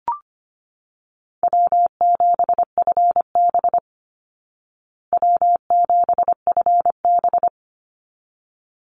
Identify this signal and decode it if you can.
Morse code, heard as a keyed tone, about 25 wpm, 700 Hz. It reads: W7FB W7FB
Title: Easy Call Signs Repeated 2x - 25wpm